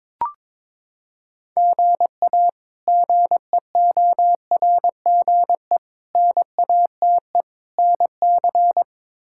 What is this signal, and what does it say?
Morse code, heard as a keyed tone, about 22 wpm, 700 Hz. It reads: GA GEORGE NATE NC
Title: K1USN SST QSO Practice - 22wpm